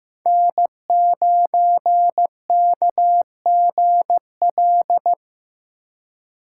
Morse code, heard as a keyed tone, about 15 wpm, 700 Hz. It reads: N9KGL
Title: USA Call Signs - 5 Characters - 2x Character Spacing - 15wpm